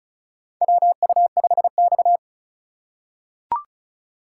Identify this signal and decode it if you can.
Morse code, heard as a keyed tone, about 36 wpm, 700 Hz. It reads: WU5X
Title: USA Call Signs - 4 Characters - 4x Character Spacing - 35wpm